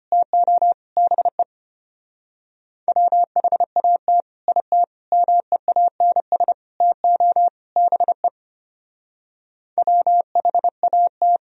Morse code, heard as a keyed tone, about 31 wpm, 700 Hz. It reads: TO BE WHAT IT MEANS TO BE WHAT
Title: Speed-Racing - Sets of 5 Words 25wpm